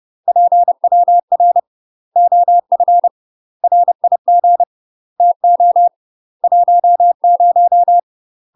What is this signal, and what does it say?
Morse code, heard as a keyed tone, about 30 wpm, 700 Hz. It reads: PWR OF RIG TO 10
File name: UK MASTER QSOs TRAINING / 30WPM / UK MASTER QSOs 0026